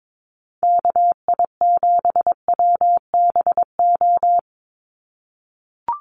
Morse code, heard as a keyed tone, about 22 wpm, 700 Hz. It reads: KI7WBO
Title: USA Call Signs 22wpm